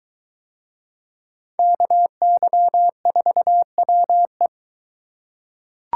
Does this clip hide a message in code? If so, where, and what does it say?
Yes: Morse code from 1.6 to 4.5 s, heard as a keyed tone, about 22 wpm, 700 Hz. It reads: KY4WE